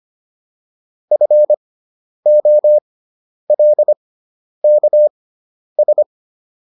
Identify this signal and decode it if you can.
Morse code, heard as a keyed tone, about 25 wpm, spaced as for 10 wpm, 600 Hz. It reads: FOLKS